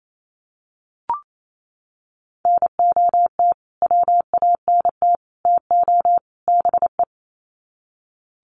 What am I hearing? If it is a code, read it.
Morse code, heard as a keyed tone, about 28 wpm, 700 Hz. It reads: NOT WANT TO BE
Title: Sets of 4 Words 28wpm